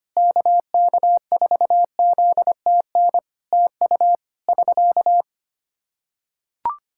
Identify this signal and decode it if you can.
Morse code, heard as a keyed tone, about 25 wpm, 700 Hz. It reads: KK4ZTN TU <SK>
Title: Practice QSO - Exchange 4 - Ending - 25wpm